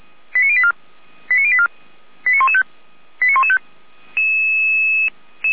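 Five-tone selective call (ZVEI), heard as quick runs of five tones, tones of about 70 ms. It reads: 89084 89084 89115 89115
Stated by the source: ZVEI five-tone call received on SDR, from feuerwehr-orlamuende fire service server.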